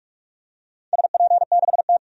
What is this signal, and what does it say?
Morse code, heard as a keyed tone, about 46 wpm, 700 Hz. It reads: SP6T
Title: DXCC-100 Call Signs - 4 Characters - 45wpm